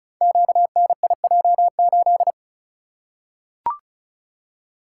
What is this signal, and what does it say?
Morse code, heard as a keyed tone, about 36 wpm, 700 Hz. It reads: QNIJ8
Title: Code Groups - 5 Letter:Numbers - 35wpm